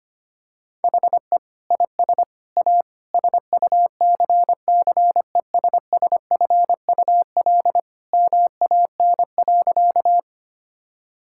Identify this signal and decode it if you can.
Morse code, heard as a keyed tone, about 25 wpm, 700 Hz. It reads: HE IS A SUCCESSFUL MAN.
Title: Sentences from Top 1000 Words - Review Part 6 - 25wpm